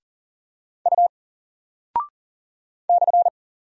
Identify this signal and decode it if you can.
Morse code, heard as a keyed tone, about 40 wpm, 700 Hz. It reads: U /
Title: Single Character (Letter:Number:Punctuation:Prosign) Alt 40wpm